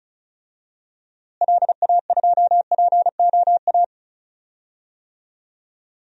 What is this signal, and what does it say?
Morse code, heard as a keyed tone, about 36 wpm, 700 Hz. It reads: LA2POA
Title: DXCC-100 Call Signs - 6 Characters - Repeated 2x - 35wpm